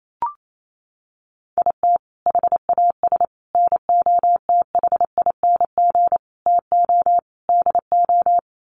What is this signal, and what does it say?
Morse code, heard as a keyed tone, about 28 wpm, 700 Hz. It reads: IT HAS NOTHING TO DO
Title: Sets of 5 Words Encore 28wpm